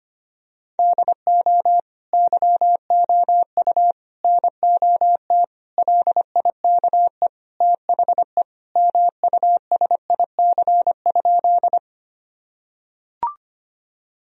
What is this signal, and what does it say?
Morse code, heard as a keyed tone, about 25 wpm, 700 Hz. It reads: DO YOU NOT LIKE THE MUSIC?
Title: Sentences from Top 1000 Words - Review Part 1 - 25wpm